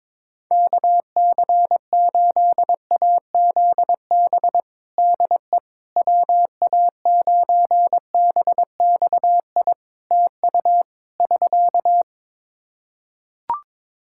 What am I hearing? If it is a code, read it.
Morse code, heard as a keyed tone, about 22 wpm, 700 Hz. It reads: KC8AZB DE WA9BXI TU <SK>
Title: K1USN SST QSO Practice 22wpm